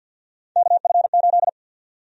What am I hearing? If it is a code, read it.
Morse code, heard as a keyed tone, about 52 wpm, 700 Hz. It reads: KF8
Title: USA Call Sign Prefixes - 50wpm